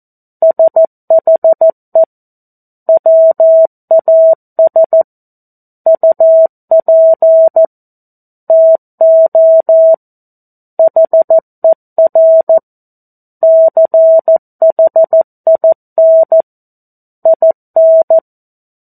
Morse code, heard as a keyed tone, about 14 wpm, 650 Hz. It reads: SHE WAS UP TO HER CHIN IN
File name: 02-Chapter-02 14 WPM